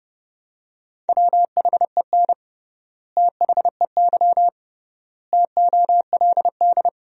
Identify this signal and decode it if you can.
Morse code, heard as a keyed tone, about 30 wpm, 700 Hz. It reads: WHEN THEY TOLD
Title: 3x Word-Spacing - Sentences from Top 400 Words - 30wpm